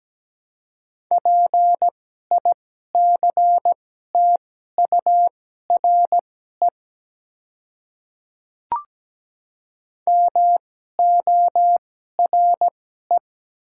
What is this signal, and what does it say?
Morse code, heard as a keyed tone, about 17 wpm, spaced as for 12 wpm, 700 Hz. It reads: PICTURE MORE
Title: 2x Character Spacing - Top 200 Words - 17wpm